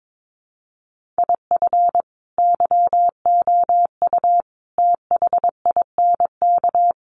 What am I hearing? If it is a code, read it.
Morse code, heard as a keyed tone, about 22 wpm, 700 Hz. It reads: IF YOU THINK
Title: Sets of 5 Words Encore2 22wpm